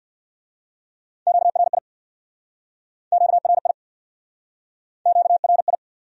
Morse code, heard as a keyed tone, about 62 wpm, 700 Hz. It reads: QRI QRI QRI
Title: Speed-Racing Q-Codes - 50wpm